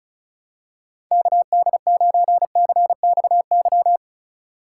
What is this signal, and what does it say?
Morse code, heard as a keyed tone, about 34 wpm, 700 Hz. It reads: KD9CXY
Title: USA Call Signs - 6 Characters - 2x Character Spacing - 35wpm